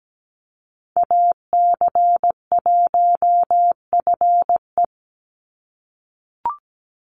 Morse code, heard as a keyed tone, about 17 wpm, 700 Hz. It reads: AC1FE
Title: Call Signs - 17wpm